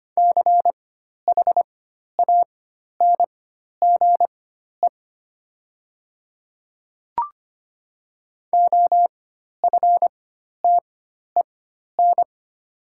Morse code, heard as a keyed tone, about 25 wpm, spaced as for 12 wpm, 690 Hz. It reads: CHANGE OFTEN
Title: 4x Character Spacing - Top 300 Words - 25wpm